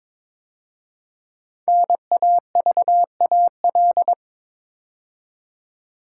Morse code, heard as a keyed tone, about 22 wpm, 700 Hz. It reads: NAVAL
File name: 5 Letter Words - First 3 Letters Given - 22wpm